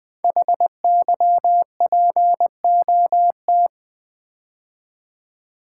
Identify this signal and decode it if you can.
Morse code, heard as a keyed tone, about 20 wpm, 700 Hz. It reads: HYPOT
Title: Build A Word - 5 Letters At A Time - 10-Letter Words - 20wpm